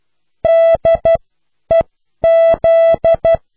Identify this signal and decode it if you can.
Morse code, heard as a keyed tone, about 12 wpm, spaced as for 10 wpm, 660 Hz. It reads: DEZ